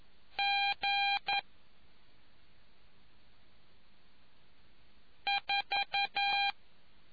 Morse code, heard as a keyed tone, about 11 wpm, 800 Hz. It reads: G 4